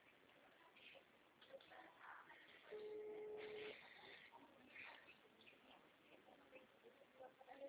Ringback tone (Europe, 425 Hz), with noise.
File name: out- -2009- - - .48888